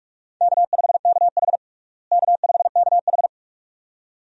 Morse code, heard as a keyed tone, about 41 wpm, 700 Hz. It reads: K5KH K5KH
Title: USA Call Signs - 4 Characters - Repeated 2x - 45wpm